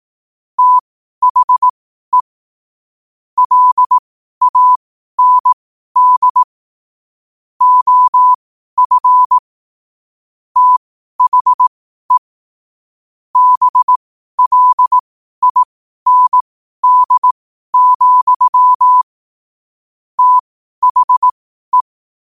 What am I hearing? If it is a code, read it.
Morse code, heard as a keyed tone, about 18 wpm, spaced as for 12 wpm, 1000 Hz. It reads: THE LAND OF THE BLIND, THE
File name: qotdfiles / 12wpm